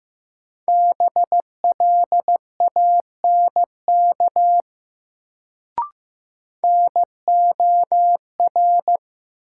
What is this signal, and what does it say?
Morse code, heard as a keyed tone, about 15 wpm, 700 Hz. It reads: BLANK NOR
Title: Sentences from Top 1500 Words - New Word Drill - 15wpm